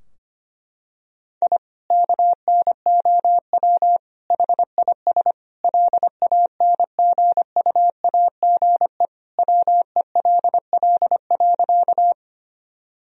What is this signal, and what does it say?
Morse code, heard as a keyed tone, about 25 wpm, 700 Hz. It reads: I KNOW HIS LANGUAGE WELL.
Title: Sentences from Top 1000 Words - Review Part 3 - 25wpm